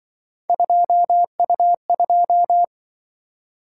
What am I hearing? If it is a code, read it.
Morse code, heard as a keyed tone, about 24 wpm, 700 Hz. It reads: 2U2
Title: Speed-Racing - Three Letter-Number Combinations - 20wpm